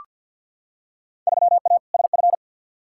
Morse code, heard as a keyed tone, about 48 wpm, 700 Hz. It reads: 3A SF
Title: Field Day Exchanges Encore - 50wpm